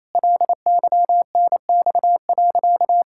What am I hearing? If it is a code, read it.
Morse code, heard as a keyed tone, about 28 wpm, 700 Hz. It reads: LYNX.
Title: Pangram Challenge v2 - 28wpm